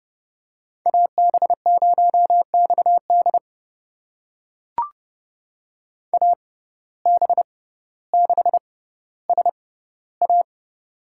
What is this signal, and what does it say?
Morse code, heard as a keyed tone, about 31 wpm, spaced as for 16 wpm, 700 Hz. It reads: AB0XD AB6SA
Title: USA Call Signs - 5 Characters - 6x Character Spacing - 30wpm